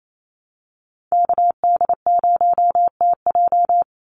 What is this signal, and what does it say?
Morse code, heard as a keyed tone, about 28 wpm, 700 Hz. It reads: KD0TJ
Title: USA Call Signs 28wpm